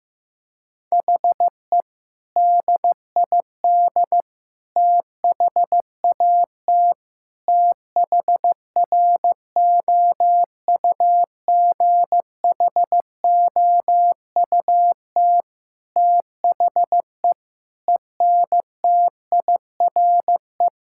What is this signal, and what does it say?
Morse code, heard as a keyed tone, about 15 wpm, 700 Hz. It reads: HE DID THAT THROUGHOUT THE ENTIRE
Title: Sentences from Top 1000 Words - Review Part 5 - 15wpm